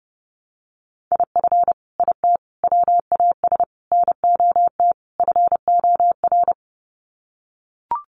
Morse code, heard as a keyed tone, about 30 wpm, 700 Hz. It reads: IF IT WAS NOT FOR
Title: Sets of 5 Words Encore 30wpm